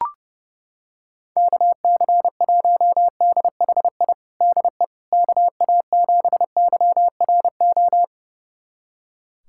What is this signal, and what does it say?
Morse code, heard as a keyed tone, about 32 wpm, 700 Hz. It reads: KC1DHI DE KA7YRO
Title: USA Callsign DE Callsign - 6 Characters - 30wpm